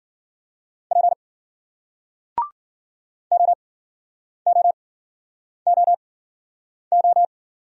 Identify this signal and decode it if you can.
Morse code, heard as a keyed tone, about 50 wpm, spaced as for 9 wpm, 700 Hz. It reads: P OOOO